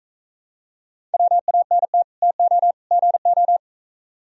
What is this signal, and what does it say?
Morse code, heard as a keyed tone, about 45 wpm, 700 Hz. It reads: WANT TO GO